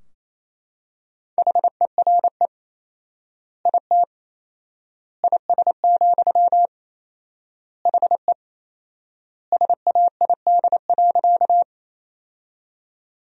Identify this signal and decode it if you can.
Morse code, heard as a keyed tone, about 28 wpm, 700 Hz. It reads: HERE IT IS, HE SAID.
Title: 4x Word-Spacing - Sentences from Top 200 Words - 28wpm